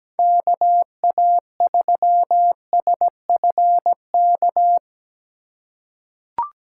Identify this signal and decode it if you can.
Morse code, heard as a keyed tone, about 17 wpm, 700 Hz. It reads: KA3SFK
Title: USA Call Signs - 6 Characters - 2x Character Spacing - 17wpm